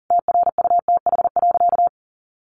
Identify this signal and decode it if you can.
Morse code, heard as a keyed tone, about 40 wpm, 700 Hz. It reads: TRUTH.
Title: Sentences from Top 1100 Words 40wpm